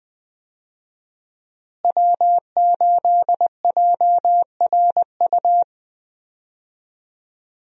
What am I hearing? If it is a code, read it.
Morse code, heard as a keyed tone, about 21 wpm, 700 Hz. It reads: W8JRU
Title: USA Call Signs - 5 Characters - Repeated 3x - 20wpm